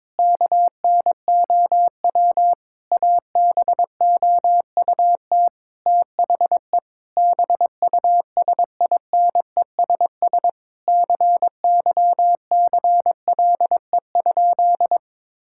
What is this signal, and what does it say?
Morse code, heard as a keyed tone, about 22 wpm, 700 Hz. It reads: KNOW ABOUT THE BUSINESS CYCLE?
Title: Sentences from Top 1000 Words - Review Part 6 - 22wpm